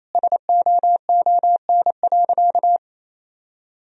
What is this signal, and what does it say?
Morse code, heard as a keyed tone, about 28 wpm, 700 Hz. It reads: SOON.
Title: Sentences from Top 400 Words and Abbreviations 28wpm